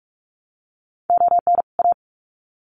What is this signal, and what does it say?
Morse code, heard as a keyed tone, about 44 wpm, 700 Hz. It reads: ON A
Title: Sets of 2 Words 45wpm